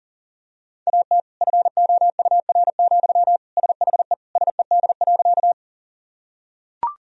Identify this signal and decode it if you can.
Morse code, heard as a keyed tone, about 40 wpm, 700 Hz. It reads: AT FOUR, SHE SED.
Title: Sentences from Top 300 Words and Abbreviations 40wpm